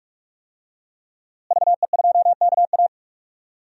Morse code, heard as a keyed tone, about 44 wpm, 700 Hz. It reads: VE2KA
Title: DXCC-100 Call Signs - 5 Characters - Repeated 2x - 45wpm